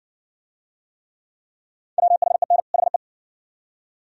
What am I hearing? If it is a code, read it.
Morse code, heard as a keyed tone, about 60 wpm, 700 Hz. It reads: WHEN HE